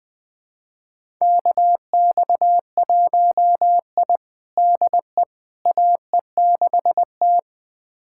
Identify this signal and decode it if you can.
Morse code, heard as a keyed tone, about 20 wpm, 700 Hz. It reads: KX1I DE AE6T
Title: USA Callsign DE Callsign - 4 Characters - 20wpm